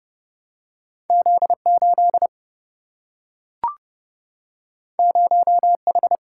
Morse code, heard as a keyed tone, about 30 wpm, 700 Hz. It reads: Z 8 0 H
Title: Single-Letter Single-Number 30wpm